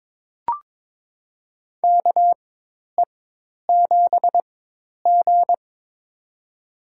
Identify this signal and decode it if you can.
Morse code, heard as a keyed tone, about 22 wpm, spaced as for 10 wpm, 700 Hz. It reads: KE7G
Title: USA Call Signs - 4 Characters - 4x Character Spacing - 22wpm